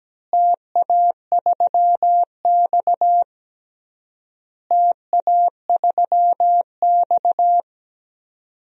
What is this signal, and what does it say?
Morse code, heard as a keyed tone, about 17 wpm, 700 Hz. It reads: TA3X TA3X
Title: DXCC-100 Call Signs - 4 Characters - Repeated 3x - 17wpm